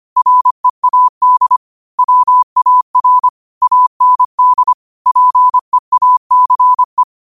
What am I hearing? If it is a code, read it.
Morse code, heard as a keyed tone, about 25 wpm, 1000 Hz. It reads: READ WAR AND PEACE
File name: 25wpm